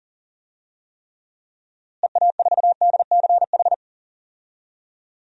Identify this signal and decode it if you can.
Morse code, heard as a keyed tone, about 40 wpm, 700 Hz. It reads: EA4DCH
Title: DXCC-100 Call Signs - 6 Characters - Repeated 2x - 40wpm